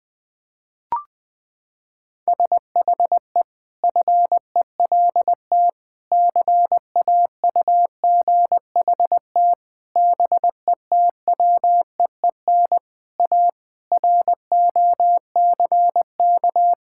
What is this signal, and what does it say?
Morse code, heard as a keyed tone, about 21 wpm, 700 Hz. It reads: SHE FELT CAUGHT BETWEEN A ROCK
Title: Sentences with 8-Word Phrases - 20wpm